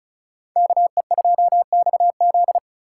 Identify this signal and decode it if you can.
Morse code, heard as a keyed tone, about 36 wpm, 700 Hz. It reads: KE2XZ
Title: USA Call Signs - 5 Characters - 4x Character Spacing - 35wpm